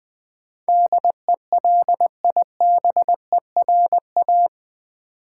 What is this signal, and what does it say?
Morse code, heard as a keyed tone, about 20 wpm, 700 Hz. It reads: DELIBERA
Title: Build A Word - 4 Letters At A Time - 12-Letter Words - 20wpm